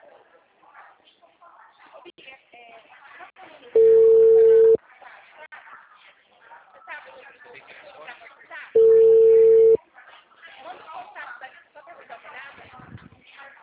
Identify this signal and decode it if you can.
Ringback tone (China, 450 Hz), with speech.
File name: out- -2022